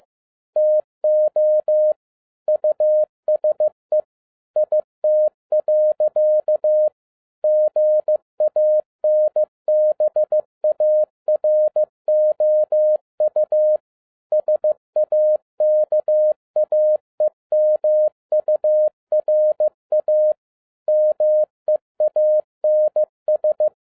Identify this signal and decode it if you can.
Morse code, heard as a keyed tone, about 15 wpm, 610 Hz. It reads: TO USE IT. GANBAROU SAKAEMURA MEANS
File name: 2012-01-mp3 / n13-A NEWLY DESIGNED QSL CARD-15wpm